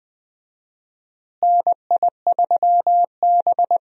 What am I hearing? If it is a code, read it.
Morse code, heard as a keyed tone, about 20 wpm, 700 Hz. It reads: NI3B
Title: USA Call Signs - 4 Characters - Repeated 2x - 20wpm